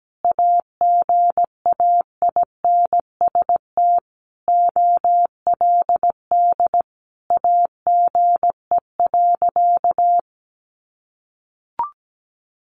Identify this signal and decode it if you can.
Morse code, heard as a keyed tone, about 17 wpm, 700 Hz. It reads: AGAINST OLD AGE.
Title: Sentences from Top 1100 Words 17wpm